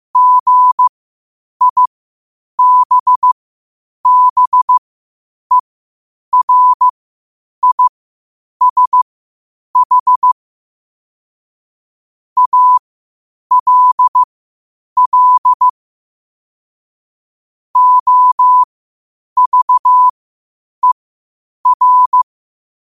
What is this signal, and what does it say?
Morse code, heard as a keyed tone, about 15 wpm, spaced as for 8 wpm, 1000 Hz. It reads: GIBBERISH ALL OVER
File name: qotdfiles / 8wpm